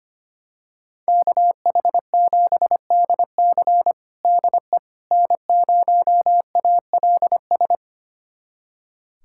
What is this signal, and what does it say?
Morse code, heard as a keyed tone, about 25 wpm, 700 Hz. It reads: KH7DC DE N0ALS
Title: USA Callsign DE Callsign - 5 Characters - 25wpm